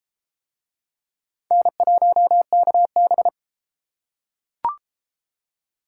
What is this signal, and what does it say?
Morse code, heard as a keyed tone, about 30 wpm, 700 Hz. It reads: N1KB